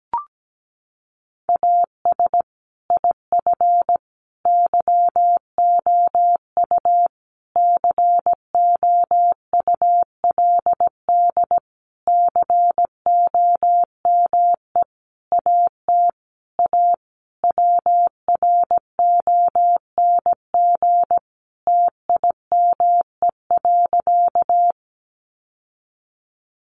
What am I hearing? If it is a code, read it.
Morse code, heard as a keyed tone, about 17 wpm, 700 Hz. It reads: AS IF YOU COULD COME AT A WRONG TIME.
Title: Sentences from Top 900 Words 17wpm